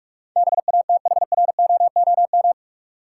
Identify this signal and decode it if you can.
Morse code, heard as a keyed tone, about 46 wpm, 700 Hz. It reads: BATHROOM
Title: Sets of 5 Words Encore - New Word Drill - 45wpm